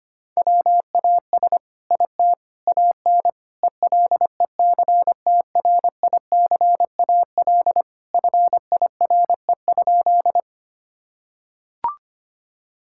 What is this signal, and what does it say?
Morse code, heard as a keyed tone, about 25 wpm, 700 Hz. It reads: WAS IT AN ELECTRICAL FIRE?
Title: Sentences from Top 1000 Words - Review Part 6 - 25wpm